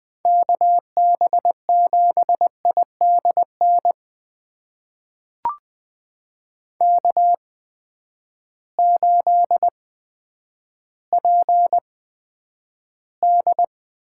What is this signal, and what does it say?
Morse code, heard as a keyed tone, about 20 wpm, spaced as for 10 wpm, 700 Hz. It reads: KB7IDN K8PD
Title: USA Call Signs - 8x Character Spacing - 20wpm